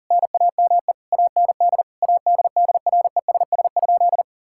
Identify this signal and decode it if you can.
Morse code, heard as a keyed tone, about 40 wpm, 700 Hz. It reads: NAME AND ADDRESS?
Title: Sentences from Top 1000 Words - Review Part 6 - 40wpm